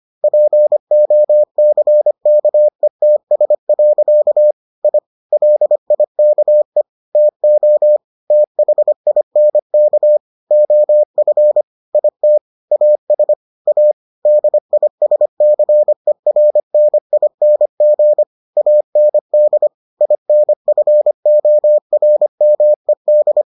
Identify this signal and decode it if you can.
Morse code, heard as a keyed tone, about 25 wpm, 600 Hz. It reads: POCKETS. I LIKE TO THINK OF IT AS A DISCERNING AND INFORMED